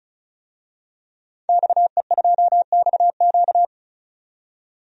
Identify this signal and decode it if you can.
Morse code, heard as a keyed tone, about 34 wpm, 700 Hz. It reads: XE2XQ
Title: DXCC-100 Call Signs - 5 Characters - 35wpm